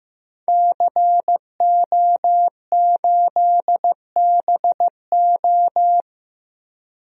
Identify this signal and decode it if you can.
Morse code, heard as a keyed tone, about 15 wpm, 700 Hz. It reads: CO8BO
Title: DXCC-100 Call Signs - 5 Characters - 15wpm